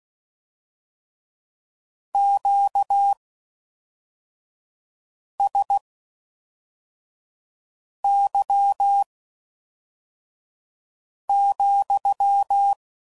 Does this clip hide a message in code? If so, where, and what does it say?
Yes: Morse code from 2.1 to 12.7 s, heard as a keyed tone, about 16 wpm, spaced as for 4 wpm, 800 Hz. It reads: QSY,